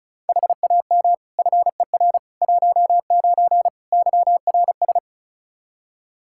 Morse code, heard as a keyed tone, about 34 wpm, 700 Hz. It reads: HAM FER 19 YRS
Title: Practice QSO - Exchange 3 - More Info - 35wpm